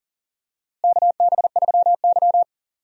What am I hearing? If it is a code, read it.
Morse code, heard as a keyed tone, about 40 wpm, 700 Hz. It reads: KB3Y
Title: USA Call Signs - 4 Characters - Repeated 3x - 40wpm